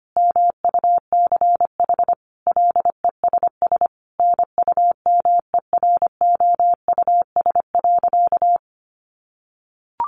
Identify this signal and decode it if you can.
Morse code, heard as a keyed tone, about 25 wpm, 700 Hz. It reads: MUCH LESS NUMEROUS.
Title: Sentences from Top 1400 Words 25wpm